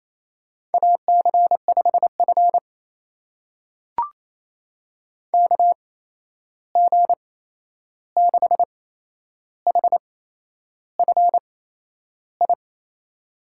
Morse code, heard as a keyed tone, about 28 wpm, spaced as for 11 wpm, 700 Hz. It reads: AC5F KG6HFI
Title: USA Call Signs - 8x Character Spacing - 28wpm